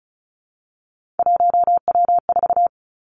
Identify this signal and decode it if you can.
Morse code, heard as a keyed tone, about 36 wpm, 700 Hz. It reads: 1W4
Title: Three Challenging Letter-Number Combinations 35wpm